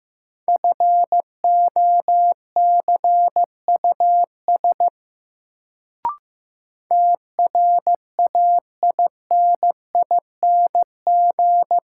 Morse code, heard as a keyed tone, about 15 wpm, 700 Hz. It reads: FOCUS TRAINING
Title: Sentences from Top 1000 Words - Review Part 4 - New Word Drill - 15wpm